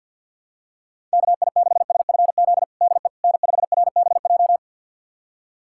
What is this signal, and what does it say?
Morse code, heard as a keyed tone, about 52 wpm, 700 Hz. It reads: KI6SFZ DE N5RBJ
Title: USA Callsign DE Callsign 50wpm